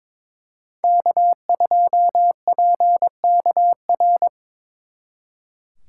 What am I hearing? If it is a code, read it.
Morse code, heard as a keyed tone, about 25 wpm, 700 Hz. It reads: K2PKR
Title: Speed Racing - USA 1x3 Call Signs 22wpm